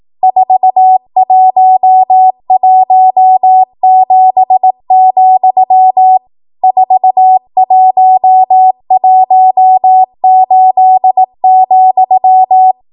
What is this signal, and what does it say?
Morse code, heard as a keyed tone, about 18 wpm, 750 Hz. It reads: 4117, 4118,
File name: Morse / 18WPM